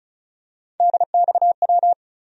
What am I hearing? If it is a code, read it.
Morse code, heard as a keyed tone, about 34 wpm, 700 Hz. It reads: DXW
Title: USA Call Sign Postfixes - 3 Letters 35wpm